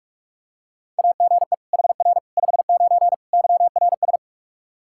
Morse code, heard as a keyed tone, about 44 wpm, 700 Hz. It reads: AGE HR 59 YRS
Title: Practice QSO - Exchange 3 - More Info - 45wpm